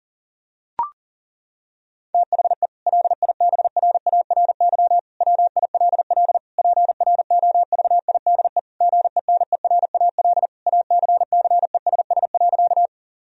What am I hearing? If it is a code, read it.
Morse code, heard as a keyed tone, about 40 wpm, 700 Hz. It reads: THE LIBRARY WILL PROVIDE GENERAL ACCESS.
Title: Sentences from Top 1000 Words - Review Part 4 - 40wpm